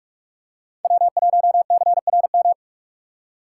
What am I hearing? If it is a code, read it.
Morse code, heard as a keyed tone, about 46 wpm, 700 Hz. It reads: W1CRM
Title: USA Call Signs - 5 Characters - 6x Character Spacing - 45wpm